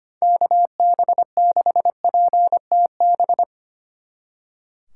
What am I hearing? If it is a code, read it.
Morse code, heard as a keyed tone, about 30 wpm, 700 Hz. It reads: KB6PTB
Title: Speed Racing - USA 2x3 Call Signs 25wpm